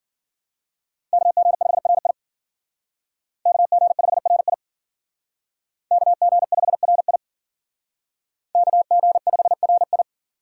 Morse code, heard as a keyed tone, about 51 wpm, 700 Hz. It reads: KG5RI KG5RI KG5RI KG5RI